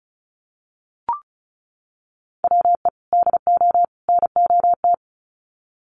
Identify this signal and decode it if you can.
Morse code, heard as a keyed tone, about 34 wpm, 700 Hz. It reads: WE DO NOT
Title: Sets of 3 Words 35wpm